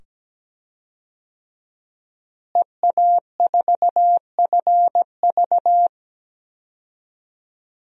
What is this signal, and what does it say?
Morse code, heard as a keyed tone, about 17 wpm, 700 Hz. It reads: EA4FV
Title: DXCC-100 Call Signs - 5 Characters - Repeated 2x - 17wpm